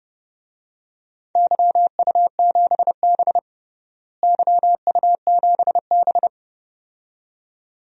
Morse code, heard as a keyed tone, about 30 wpm, 700 Hz. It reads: YU7B YU7B
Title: DXCC-100 Call Signs - 4 Characters - Repeated 2x - 30wpm